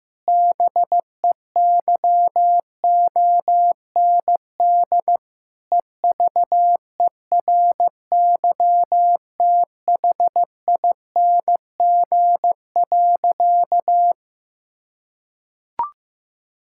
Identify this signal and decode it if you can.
Morse code, heard as a keyed tone, about 15 wpm, 700 Hz. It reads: BEYOND EVERYTHING.
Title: Sentences from Top 1000 Words - Review Part 6 - 15wpm